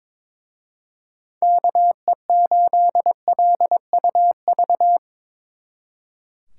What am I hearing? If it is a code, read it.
Morse code, heard as a keyed tone, about 24 wpm, 700 Hz. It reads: KE8LUV
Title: Speed Racing - USA 2x3 Call Signs 22wpm